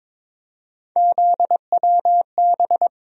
Morse code, heard as a keyed tone, about 22 wpm, 700 Hz. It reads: ZWB
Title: USA Call Sign Postfixes - 3 Letters 22wpm